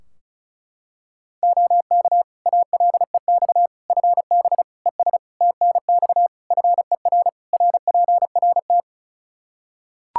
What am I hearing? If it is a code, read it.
Morse code, heard as a keyed tone, about 36 wpm, 700 Hz. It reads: OK ALEX FB ES TNX FER RPRT
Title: Practice QSO - Exchange 2 - Info - 35wpm